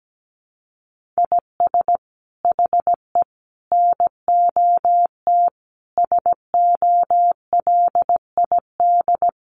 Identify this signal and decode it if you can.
Morse code, heard as a keyed tone, about 17 wpm, 700 Hz. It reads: IS HE NOT SOLID
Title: Sentences from Top 1500 Words 17wpm